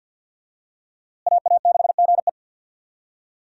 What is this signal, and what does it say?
Morse code, heard as a keyed tone, about 51 wpm, 700 Hz. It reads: AA6GE
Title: USA Call Signs - 5 Characters - Repeated 3x - 50wpm